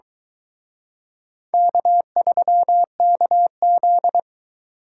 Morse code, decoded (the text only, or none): K3KZ